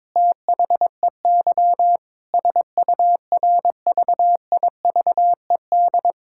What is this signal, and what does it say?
Morse code, heard as a keyed tone, about 22 wpm, 700 Hz. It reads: THEY SURVIVED
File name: Sentences with 7-Word Phrases - 22wpm